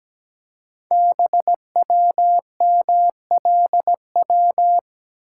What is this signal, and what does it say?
Morse code, heard as a keyed tone, about 17 wpm, 700 Hz. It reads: BWMLW
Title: Code Groups - 5 Letters - 17wpm